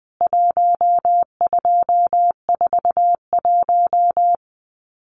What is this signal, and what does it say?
Morse code, heard as a keyed tone, about 20 wpm, 700 Hz. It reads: 1241
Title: CWT Exchange 20wpm